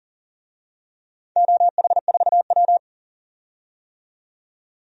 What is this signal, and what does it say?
Morse code, heard as a keyed tone, about 40 wpm, 700 Hz. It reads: OH4W
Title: DXCC-100 Call Signs - 4 Characters - 40wpm